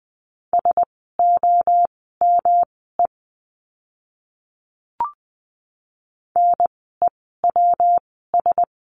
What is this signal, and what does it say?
Morse code, heard as a keyed tone, about 20 wpm, spaced as for 14 wpm, 700 Hz. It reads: SOME NEWS
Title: 115 Lesson 19f - Letter w - Words 20wpm